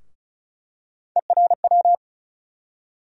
Morse code, heard as a keyed tone, about 34 wpm, 700 Hz. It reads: ERW
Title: USA Call Sign Postfixes - 3 Letters 35wpm